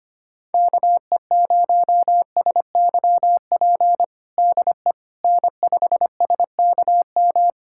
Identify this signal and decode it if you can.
Morse code, heard as a keyed tone, about 25 wpm, 700 Hz. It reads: KE0SYP DE N5SKM
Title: USA Callsign DE Callsign 25wpm